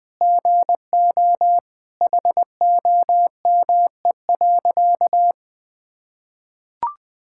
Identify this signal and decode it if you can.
Morse code, heard as a keyed tone, about 20 wpm, 700 Hz. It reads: GO HOME.
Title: Sentences with 3-Word Phrases - 20wpm